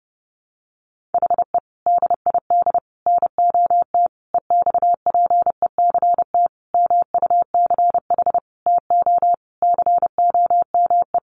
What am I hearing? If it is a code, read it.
Morse code, heard as a keyed tone, about 30 wpm, 700 Hz. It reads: HE DID NOT EXPECT MUCH TO COME
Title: Sentences from Top 1100 Words 30wpm